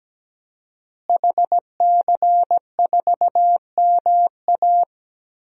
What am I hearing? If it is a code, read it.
Morse code, heard as a keyed tone, about 17 wpm, 700 Hz. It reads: HC4MA
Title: DXCC-100 Call Signs - 5 Characters - Repeated 2x - 17wpm